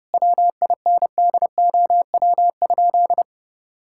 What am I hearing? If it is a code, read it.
Morse code, heard as a keyed tone, about 30 wpm, 700 Hz. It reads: WINDOW?
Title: Sentences from Top 1600 Words 30wpm